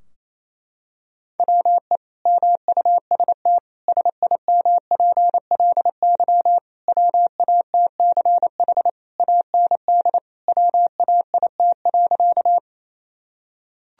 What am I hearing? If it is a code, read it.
Morse code, heard as a keyed tone, about 28 wpm, 700 Hz. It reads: WE MUST SIMPLY WATCH AND WAIT.
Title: Sentences from Top 1000 Words - Review Part 4 - 28wpm